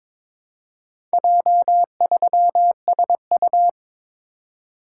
Morse code, heard as a keyed tone, about 22 wpm, 700 Hz. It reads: J3SU
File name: practice / 22wpm / Four Letter-Number Combinations - 22wpm